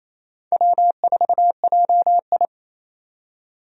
Morse code, heard as a keyed tone, about 28 wpm, 700 Hz. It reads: W4JI
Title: USA Call Signs - 4 Characters - 28wpm